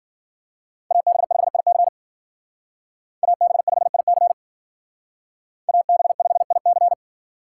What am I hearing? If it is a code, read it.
Morse code, heard as a keyed tone, about 50 wpm, 700 Hz. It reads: AB5IC AB5IC AB5IC
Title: Speed Racing - USA 2x2 Call Signs 40wpm